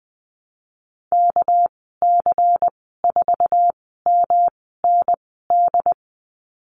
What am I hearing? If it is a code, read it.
Morse code, heard as a keyed tone, about 20 wpm, spaced as for 15 wpm, 700 Hz. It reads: KC4MND
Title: 207 Lesson 33g - Letter j - Call Signs 20wpm